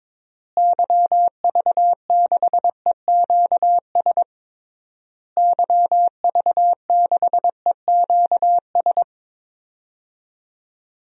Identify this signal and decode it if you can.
Morse code, heard as a keyed tone, about 22 wpm, 700 Hz. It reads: YV6EQS YV6EQS
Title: DXCC-100 Call Signs - 6 Characters - Repeated 3x - 22wpm